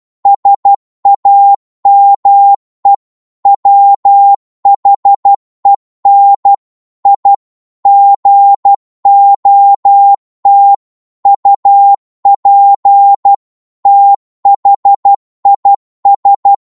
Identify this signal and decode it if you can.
Morse code, heard as a keyed tone, about 12 wpm, 800 Hz. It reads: SAME WHEN I GOT UP THIS